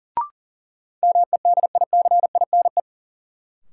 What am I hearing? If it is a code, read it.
Morse code, heard as a keyed tone, about 40 wpm, 700 Hz. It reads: MEDICINE
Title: Sentences from Top 900 Words - New Word Drill - 40wpm